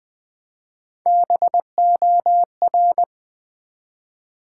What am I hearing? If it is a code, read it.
Morse code, heard as a keyed tone, about 20 wpm, 690 Hz. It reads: BOR
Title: Build A Word - 1 Letter At A Time - 4-Letter Words - 20wpm